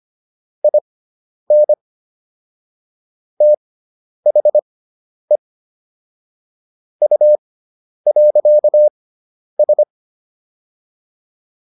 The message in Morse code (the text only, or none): IN THE U.S